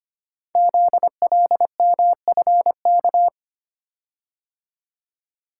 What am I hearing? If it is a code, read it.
Morse code, heard as a keyed tone, about 25 wpm, 700 Hz. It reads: ZLMFK
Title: Code Groups - 5 Letters - 25wpm